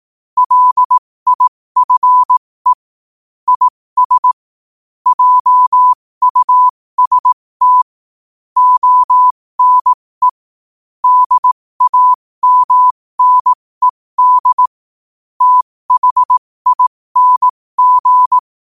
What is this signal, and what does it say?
Morse code, heard as a keyed tone, about 18 wpm, spaced as for 15 wpm, 1000 Hz. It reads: LIFE IS JUST ONE DAMNED THING
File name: qotdfiles / 15wpm